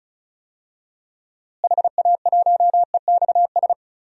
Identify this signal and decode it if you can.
Morse code, heard as a keyed tone, about 36 wpm, 700 Hz. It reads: HA1EXS
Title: DXCC-100 Call Signs - 6 Characters - Repeated 3x - 35wpm